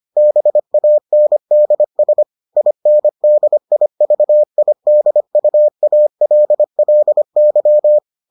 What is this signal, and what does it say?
Morse code, heard as a keyed tone, about 25 wpm, 600 Hz. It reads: BANDS INDIVIDUALLY